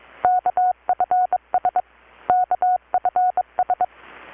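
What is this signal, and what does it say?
Morse code, heard as a keyed tone, about 22 wpm, 720 Hz. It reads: KFS KFS
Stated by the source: Morse from KPH's Night of Nights XX maritime radiotelegraphy event, received in Chiapas, Mexico about 3,800 km away on a Kenwood R-600 with a grounded 10 m long-wire antenna.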